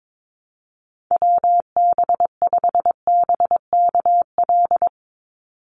Morse code, heard as a keyed tone, about 22 wpm, 700 Hz. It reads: WB5BKL
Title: NA SOTA Chaser Callsigns 22wpm